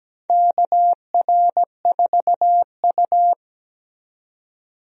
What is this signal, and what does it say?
Morse code, heard as a keyed tone, about 17 wpm, 700 Hz. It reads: KR4U
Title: USA Call Signs - 4 Characters - 17wpm